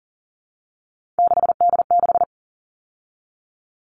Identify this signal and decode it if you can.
Morse code, heard as a keyed tone, about 40 wpm, 700 Hz. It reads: 6D6
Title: All Cut Numbers - 3-Digit 40wpm